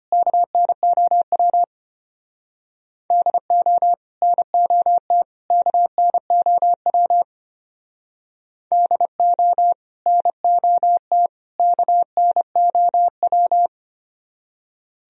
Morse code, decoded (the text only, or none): KNOW DO NOT KNOW DO NOT KNOW